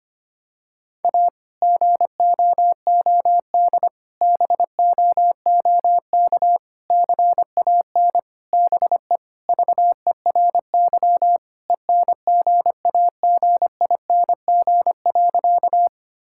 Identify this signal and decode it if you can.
Morse code, heard as a keyed tone, about 25 wpm, 700 Hz. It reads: A GOOD BOOK CAN BE VERY ENGAGING.
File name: practice / 25wpm / Sentences with 2-Word Phrases - 25wpm